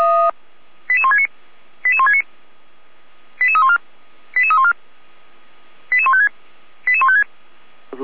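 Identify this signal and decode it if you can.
Five-tone selective call (ZVEI), heard as quick runs of five tones, tones of about 70 ms. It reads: 88169 88169 88314 88314 88156 88156
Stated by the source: Five-tone ZVEI selective call received by SDR.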